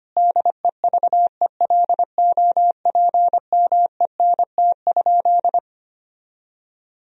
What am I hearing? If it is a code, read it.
Morse code, heard as a keyed tone, about 25 wpm, 700 Hz. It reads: DEVELOPMENT?
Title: Sentences from Top 1000 Words - Review Part 4 - 25wpm